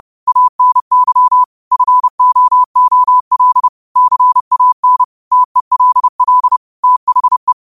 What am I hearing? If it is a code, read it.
Morse code, heard as a keyed tone, about 30 wpm, 1000 Hz. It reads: ANY FOOL CAN TELL THE
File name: qotdfiles / 30wpm